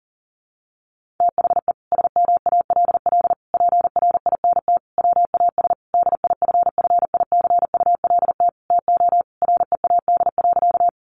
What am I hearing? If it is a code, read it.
Morse code, heard as a keyed tone, about 40 wpm, 700 Hz. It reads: THE SMALL PRINT WAS DIFFICULT TO READ.
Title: Sentences from Top 1400 Words 40wpm